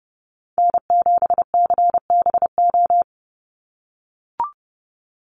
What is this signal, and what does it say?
Morse code, heard as a keyed tone, about 30 wpm, 700 Hz. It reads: N7CBO
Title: USA Call Signs 30wpm